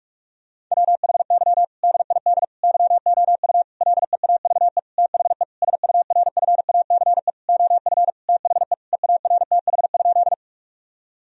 Morse code, heard as a keyed tone, about 46 wpm, 700 Hz. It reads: WHY DID YOU LEAVE THE SURFACE OF THE EARTH?
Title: Sentences from Top 1000 Words - Review Part 3 - 45wpm